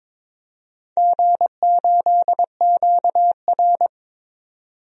Morse code, heard as a keyed tone, about 22 wpm, 700 Hz. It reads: G8QR
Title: DXCC-100 Call Signs - 4 Characters - 22wpm